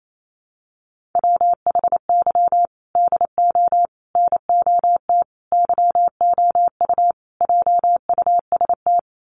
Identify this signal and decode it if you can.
Morse code, heard as a keyed tone, about 28 wpm, 700 Hz. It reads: WHY DO NOT YOU JUST
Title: Sets of 5 Words 28wpm